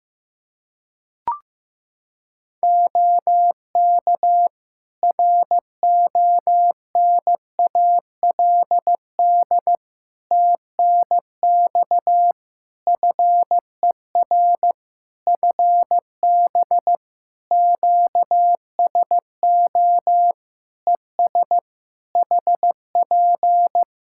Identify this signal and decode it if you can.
Morse code, heard as a keyed tone, about 15 wpm, 700 Hz. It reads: OK RONALD TNX FER FB QSO ES HP
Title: Practice QSO - Exchange 4 - Ending - 15wpm